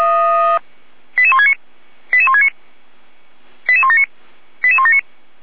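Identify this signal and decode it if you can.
Five-tone selective call (ZVEI), heard as quick runs of five tones, tones of about 70 ms. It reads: 88169 88169 88179 88179
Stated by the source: Five-tone ZVEI selective call received on SDR.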